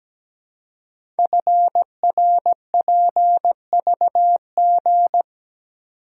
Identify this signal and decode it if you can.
Morse code, heard as a keyed tone, about 17 wpm, 700 Hz. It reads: FRPVG
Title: Code Groups - 5 Letters - 17wpm